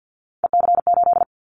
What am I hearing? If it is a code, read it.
Morse code, heard as a keyed tone, about 50 wpm, 700 Hz. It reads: EC8
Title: Callsign Prefixes for Top 50 DXCC - 50wpm